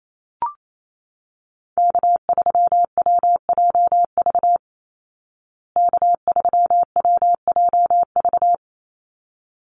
Morse code, heard as a keyed tone, about 28 wpm, 700 Hz. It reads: K3WJV K3WJV
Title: Call Signs Repeated 3x - 28wpm